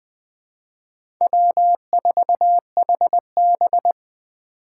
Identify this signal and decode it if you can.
Morse code, heard as a keyed tone, about 24 wpm, 700 Hz. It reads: W4HB
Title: Speed Racing - USA 1x2 Call Signs 20wpm